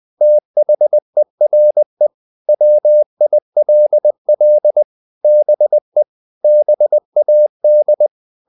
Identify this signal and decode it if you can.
Morse code, heard as a keyed tone, about 20 wpm, 600 Hz. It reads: THERE WILL BE BAD
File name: the old man and the sea part 2 20wpm